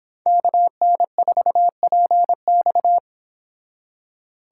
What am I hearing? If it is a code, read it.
Morse code, heard as a keyed tone, about 26 wpm, 700 Hz. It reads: KN4PX